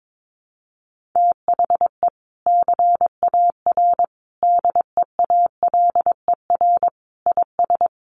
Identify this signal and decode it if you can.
Morse code, heard as a keyed tone, about 22 wpm, 700 Hz. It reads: THE CAR DEALER IS
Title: Sentences from Top 1100 Words 22wpm